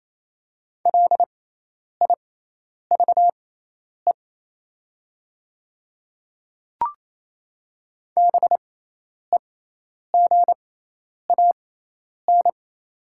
Morse code, heard as a keyed tone, about 28 wpm, spaced as for 10 wpm, 700 Hz. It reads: LIVE BEGAN
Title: 6x Character Spacing - Top 500 Words - 28wpm